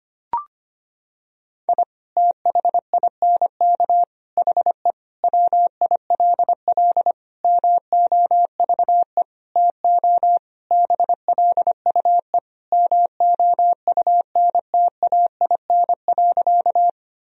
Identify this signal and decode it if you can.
Morse code, heard as a keyed tone, about 25 wpm, 700 Hz. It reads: I THINK HE WILL MOVE TO BLUE MOUNTAIN.
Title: Sentences from Top 1000 Words - Review Part 3 - 25wpm